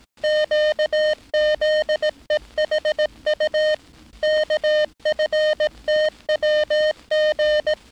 Morse code, heard as a keyed tone, about 18 wpm, 610 Hz. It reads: QZEHU KFTWG
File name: training / NAVPERS 11049to11060 / NAVPERS 11059 RA RB